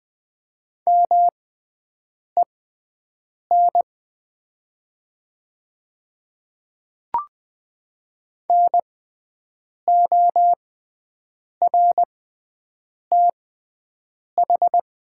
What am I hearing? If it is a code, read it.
Morse code, heard as a keyed tone, about 20 wpm, spaced as for 7 wpm, 700 Hz. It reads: MEN NORTH
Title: 6x Character Spacing - Top 500 Words - 20wpm